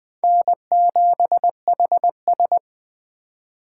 Morse code, heard as a keyed tone, about 22 wpm, 700 Hz. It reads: N7HS